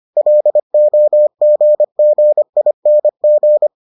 Morse code, heard as a keyed tone, about 25 wpm, 600 Hz. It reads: LOGGING